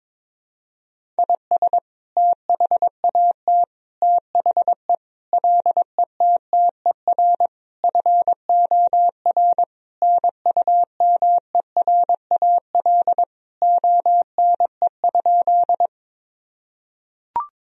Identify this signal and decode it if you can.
Morse code, heard as a keyed tone, about 22 wpm, 700 Hz. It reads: IS THAT THE LETTER FOR NUMERAL ONE?
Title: Sentences from Top 1000 Words - Review Part 2 - 22wpm